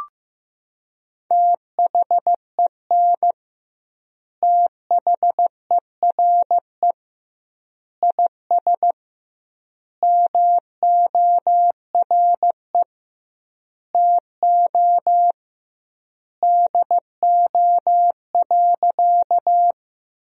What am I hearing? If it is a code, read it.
Morse code, heard as a keyed tone, about 16 wpm, 700 Hz. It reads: THEN THERE IS MORE TO DO.
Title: 2x Word-Spacing - Sentences from Top 100 Words - 15wpm